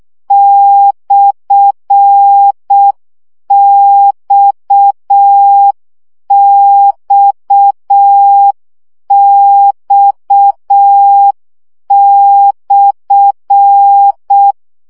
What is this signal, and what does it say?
Morse code, heard as a keyed tone, about 6 wpm, 800 Hz. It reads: /XXX/
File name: CW gyakorlas x per 5 baud